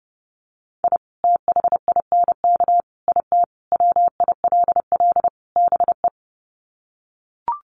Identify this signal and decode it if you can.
Morse code, heard as a keyed tone, about 30 wpm, 700 Hz. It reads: I THINK IT WILL BE
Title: Sets of 5 Words Encore2 30wpm